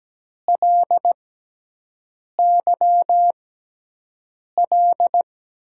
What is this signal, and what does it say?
Morse code, heard as a keyed tone, about 17 wpm, spaced as for 6 wpm, 700 Hz. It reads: LYL
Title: USA Call Signs - 6 Characters - 6x Character Spacing - 17wpm